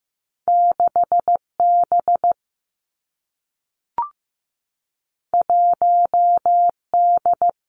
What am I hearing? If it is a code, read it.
Morse code, heard as a keyed tone, about 15 wpm, 700 Hz. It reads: 6 B 1 D
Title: Two Challenging Letter-Number Combinations 15wpm